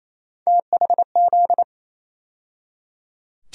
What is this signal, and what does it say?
Morse code, heard as a keyed tone, about 28 wpm, 700 Hz. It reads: THZ
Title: USA Call Sign Postfixes - 3 Letters 28wpm